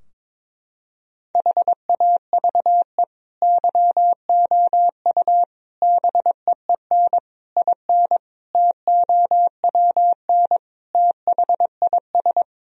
Morse code, heard as a keyed tone, about 22 wpm, 700 Hz. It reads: HAVE YOU BEEN IN TOWN THIS
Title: Sentences from Top 1000 Words - Review Part 2 - 22wpm